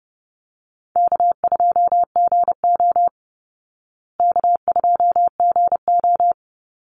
Morse code, heard as a keyed tone, about 30 wpm, 700 Hz. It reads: K2GO K2GO
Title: Easy Call Signs Repeated 3x - 30wpm